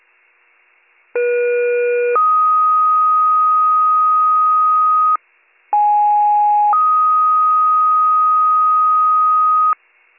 Two-tone page, heard as a one-second tone, then a three-second tone, about 480 then 1250 Hz, about 830 then 1250 Hz. It is heard from 1.2 to 9.7 s.